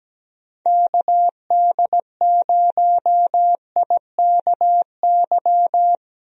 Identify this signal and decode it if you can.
Morse code, heard as a keyed tone, about 17 wpm, 700 Hz. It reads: KD0IKY
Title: USA Call Signs - 6 Characters - Repeated 3x - 17wpm